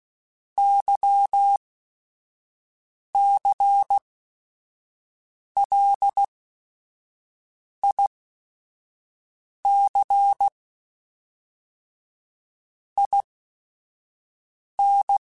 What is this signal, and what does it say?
Morse code, heard as a keyed tone, about 16 wpm, spaced as for 5 wpm, 800 Hz. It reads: YCLIC IN
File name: Technical Subjects / Technical Subjects-mp3s / XL @5